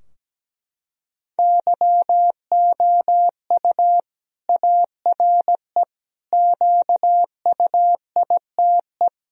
Morse code, heard as a keyed tone, about 17 wpm, 700 Hz. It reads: YOU ARE QUITE
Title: Sentences from Top 1000 Words - Review Part 2 - 17wpm